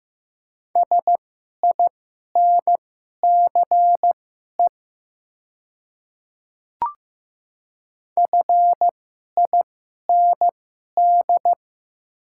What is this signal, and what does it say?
Morse code, heard as a keyed tone, about 15 wpm, spaced as for 11 wpm, 690 Hz. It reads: SINCE FIND
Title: 2x Character Spacing - Top 500 Words - 15wpm